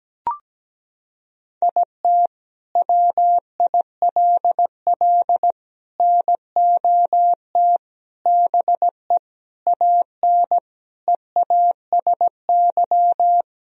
Morse code, heard as a keyed tone, about 17 wpm, 700 Hz. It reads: IT WILL NOT BE AN EASY
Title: Sentences from Top 1000 Words - Review Part 2 - 17wpm